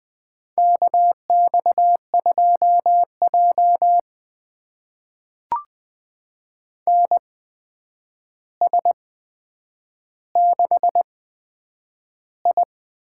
Morse code, heard as a keyed tone, about 20 wpm, spaced as for 9 wpm, 700 Hz. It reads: KX2J NS6I
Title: USA Call Signs - 4 Characters - 8x Character Spacing - 20wpm